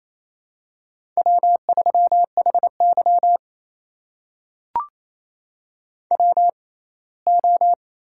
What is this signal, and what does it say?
Morse code, heard as a keyed tone, about 28 wpm, spaced as for 19 wpm, 700 Hz. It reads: W3HY WO
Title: USA Call Signs - 4 Characters - 6x Character Spacing - 28wpm